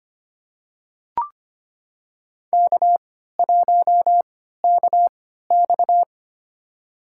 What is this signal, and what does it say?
Morse code, heard as a keyed tone, about 25 wpm, spaced as for 14 wpm, 700 Hz. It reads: K1KX
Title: USA Call Signs - 4 Characters - 3x Character Spacing - 25wpm